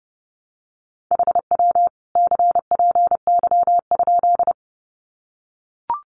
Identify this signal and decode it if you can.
Morse code, heard as a keyed tone, about 30 wpm, 700 Hz. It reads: HW CPY?
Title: Common QSO Phrases 30wpm